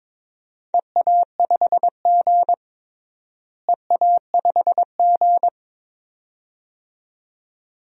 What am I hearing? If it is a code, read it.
Morse code, heard as a keyed tone, about 22 wpm, 700 Hz. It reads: EA5G EA5G
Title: DXCC-100 Call Signs - 4 Characters - Repeated 2x - 22wpm